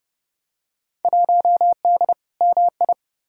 Morse code, heard as a keyed tone, about 32 wpm, 700 Hz. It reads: 1D MI
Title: Field Day Exchanges - 30wpm